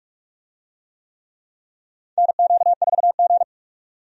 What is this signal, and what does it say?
Morse code, heard as a keyed tone, about 42 wpm, 700 Hz. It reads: NQ4G
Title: USA Call Signs - 4 Characters - 45wpm